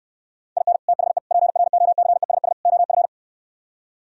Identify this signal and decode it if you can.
Morse code, heard as a keyed tone, about 68 wpm, 700 Hz. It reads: IN THE PROCESS OF